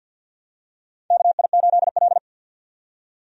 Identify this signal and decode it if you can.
Morse code, heard as a keyed tone, about 53 wpm, 700 Hz. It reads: KI8L
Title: USA Call Signs - 4 Characters - 50wpm